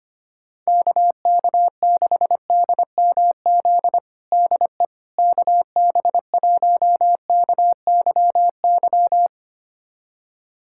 Morse code, heard as a keyed tone, about 25 wpm, 700 Hz. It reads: KK6DMZ DE KB1KYY
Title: USA Callsign DE Callsign - 6 Characters - 25wpm